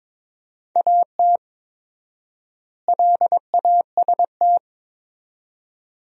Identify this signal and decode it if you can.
Morse code, heard as a keyed tone, about 22 wpm, 700 Hz. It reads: AT LAST
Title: 4x Word-Spacing - Sentences from Top 300 Words - 22wpm